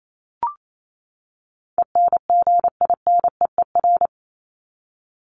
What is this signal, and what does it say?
Morse code, heard as a keyed tone, about 28 wpm, 700 Hz. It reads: ENGINEER
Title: ING Suffix - Top 100 Words with Base Words 28wpm